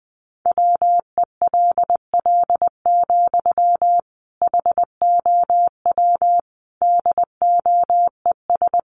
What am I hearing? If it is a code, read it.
Morse code, heard as a keyed tone, about 19 wpm, 700 Hz. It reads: WELL, HOW DOES
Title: Sentences from Top 1100 Words 20wpm